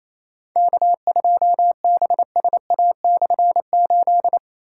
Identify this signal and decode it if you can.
Morse code, heard as a keyed tone, about 28 wpm, 700 Hz. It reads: K2BSA/8
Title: K1USN SST QSO Practice - 28wpm